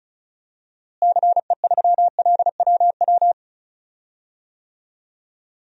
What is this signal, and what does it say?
Morse code, heard as a keyed tone, about 34 wpm, 700 Hz. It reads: CE3LWW
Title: DXCC-100 Call Signs - 6 Characters - Repeated 3x - 35wpm